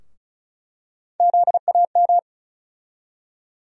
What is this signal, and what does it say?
Morse code, heard as a keyed tone, about 36 wpm, 700 Hz. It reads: ZAM
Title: USA Call Sign Postfixes - 3 Letters 35wpm